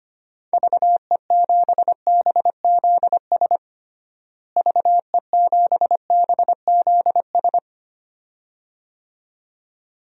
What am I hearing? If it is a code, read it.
Morse code, heard as a keyed tone, about 25 wpm, 700 Hz. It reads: VE7BZS VE7BZS
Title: DXCC-100 Call Signs - 6 Characters - Repeated 2x - 25wpm